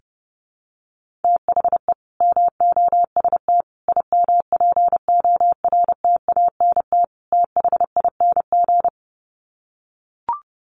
Morse code, heard as a keyed tone, about 30 wpm, 700 Hz. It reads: THE MOST IMPORTANT THING
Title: Sets of 4 Words 30wpm